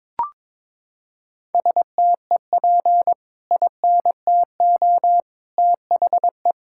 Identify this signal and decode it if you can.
Morse code, heard as a keyed tone, about 22 wpm, 700 Hz. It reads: STEP INTO THE
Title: Sentences from Top 1600 Words 22wpm